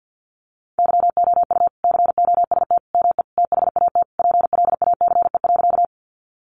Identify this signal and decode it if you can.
Morse code, heard as a keyed tone, about 50 wpm, 700 Hz. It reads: YOU COST ME THAT PLACE.
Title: Sentences from Top 800 Words 50wpm